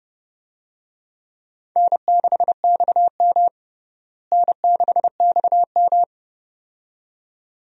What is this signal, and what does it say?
Morse code, heard as a keyed tone, about 30 wpm, 700 Hz. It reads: N6XM N6XM
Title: USA Call Signs - 4 Characters - Repeated 2x - 30wpm